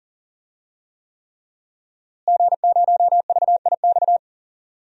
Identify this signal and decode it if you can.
Morse code, heard as a keyed tone, about 40 wpm, 700 Hz. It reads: G0VIX
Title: DXCC-100 Call Signs - 5 Characters - Repeated 3x - 40wpm